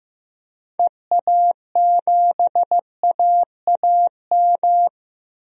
Morse code, heard as a keyed tone, about 15 wpm, 700 Hz. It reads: EA7AAM
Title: DXCC-100 Call Signs - 6 Characters - 15wpm